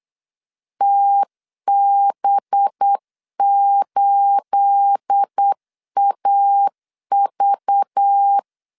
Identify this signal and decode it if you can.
Morse code, heard as a keyed tone, about 8 wpm, 800 Hz. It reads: TB8AV
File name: CW / DET / 1 9 1500C S B 8wpm 5624